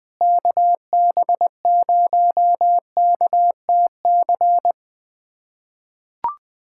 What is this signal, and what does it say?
Morse code, heard as a keyed tone, about 20 wpm, 700 Hz. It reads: KB0KTC
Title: USA Callsign DE Callsign 20wpm